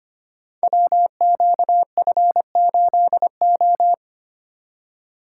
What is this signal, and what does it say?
Morse code, heard as a keyed tone, about 25 wpm, 700 Hz. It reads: WQF8O
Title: Code Groups - 5 Letter:Numbers - 25wpm